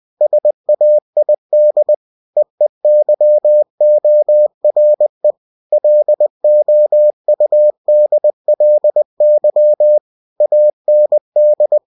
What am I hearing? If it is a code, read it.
Morse code, heard as a keyed tone, about 20 wpm, 600 Hz. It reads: SAID EEYORE LOUDLY AND
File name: cw-books / winnie-the-pooh / winnie-20wpm-page-0108